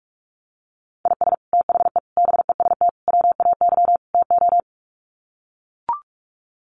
Morse code, heard as a keyed tone, about 44 wpm, 700 Hz. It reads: IS THE BEST WAY TO
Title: Sets of 5 Words Encore 45wpm